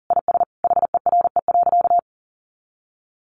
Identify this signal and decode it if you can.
Morse code, heard as a keyed tone, about 40 wpm, 700 Hz. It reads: IS HERE.
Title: Sentences from Top 600 Words 40wpm